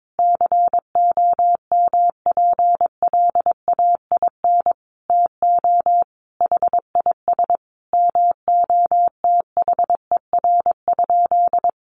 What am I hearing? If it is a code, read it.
Morse code, heard as a keyed tone, about 22 wpm, 700 Hz. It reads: COMPLAIN TO HIS MOTHER?
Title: Sentences from Top 1300 Words 22wpm